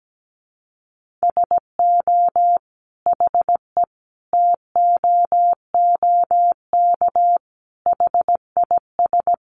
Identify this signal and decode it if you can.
Morse code, heard as a keyed tone, about 17 wpm, 700 Hz. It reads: SO HE TOOK HIS
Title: Sentences from Top 1500 Words 17wpm